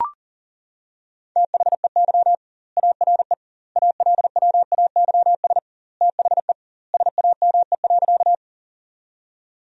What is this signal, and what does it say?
Morse code, heard as a keyed tone, about 40 wpm, 700 Hz. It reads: THEY ARE ALWAYS THE SAME.
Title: 2x Word-Spacing - Sentences from Top 300 Words - 40wpm